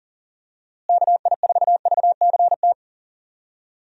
Morse code, heard as a keyed tone, about 40 wpm, 700 Hz. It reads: KI4VCT
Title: USA Call Signs - 6 Characters - Repeated 3x - 40wpm